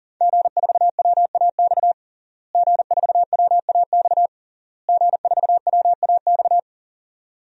Morse code, heard as a keyed tone, about 37 wpm, 700 Hz. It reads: G4WAX G4WAX G4WAX
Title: DXCC-100 Call Signs - 5 Characters - Repeated 3x - 40wpm